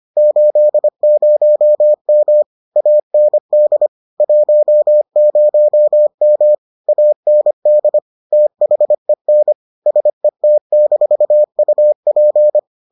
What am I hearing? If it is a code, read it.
Morse code, heard as a keyed tone, about 25 wpm, 600 Hz. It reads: 80M AND 10M AND THEN SET-UP